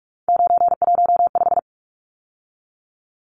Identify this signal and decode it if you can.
Morse code, heard as a keyed tone, about 44 wpm, 700 Hz. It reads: 915
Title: Numbers - 3-Digit 45wpm